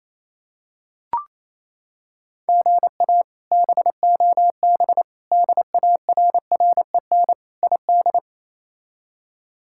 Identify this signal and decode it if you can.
Morse code, heard as a keyed tone, about 28 wpm, 700 Hz. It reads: GA BOB DARREN ID
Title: K1USN SST QSO Practice - 28wpm